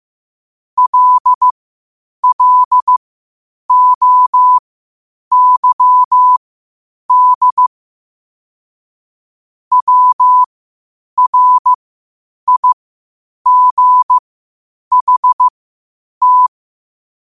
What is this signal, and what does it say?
Morse code, heard as a keyed tone, about 15 wpm, spaced as for 8 wpm, 1000 Hz. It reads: LLOYD WRIGHT
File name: qotdfiles / 8wpm